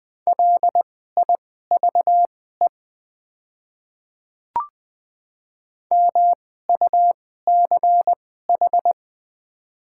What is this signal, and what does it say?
Morse code, heard as a keyed tone, about 20 wpm, spaced as for 14 wpm, 700 Hz. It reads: LIVE MUCH
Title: 2x Character Spacing - Top 200 Words - 20wpm